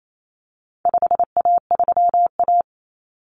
Morse code, heard as a keyed tone, about 28 wpm, 700 Hz. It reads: 5A3A
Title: Common Cut Numbers - 4-Digit 28wpm